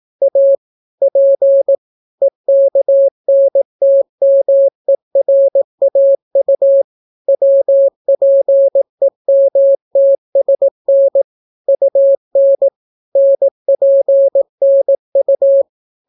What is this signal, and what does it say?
Morse code, heard as a keyed tone, about 18 wpm, 550 Hz. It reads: A P EKNTMERAU WPEMTSN UN NPNU